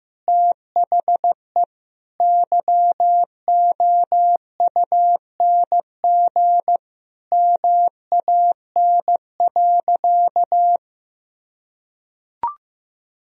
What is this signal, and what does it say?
Morse code, heard as a keyed tone, about 15 wpm, 700 Hz. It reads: THE YOUNG MAN.
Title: Sentences from Top 400 Words and Abbreviations 15wpm